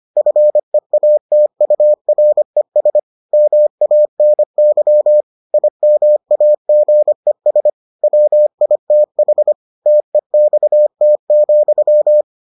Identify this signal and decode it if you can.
Morse code, heard as a keyed tone, about 25 wpm, 600 Hz. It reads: FEATURES MANY IMAGES WITH TEXT,